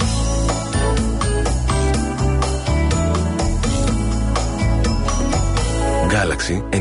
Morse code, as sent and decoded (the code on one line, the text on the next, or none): none